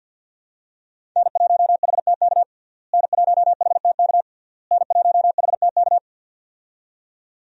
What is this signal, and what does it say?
Morse code, heard as a keyed tone, about 48 wpm, 700 Hz. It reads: N1HTK N1HTK N1HTK
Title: USA Call Signs - 5 Characters - Repeated 3x - 50wpm